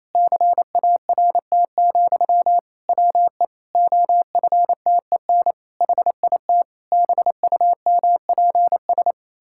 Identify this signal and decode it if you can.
Morse code, heard as a keyed tone, about 27 wpm, 700 Hz. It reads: CART, WE OFTEN HIT BUMPS
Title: Pangram Challenge v2 - 28wpm